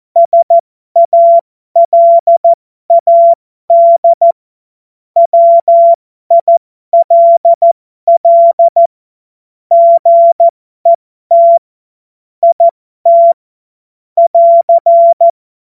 Morse code, heard as a keyed tone, about 14 wpm, spaced as for 12 wpm, 680 Hz. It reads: SALAD WILL GET IT <AR>
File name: Morsekurs / Witze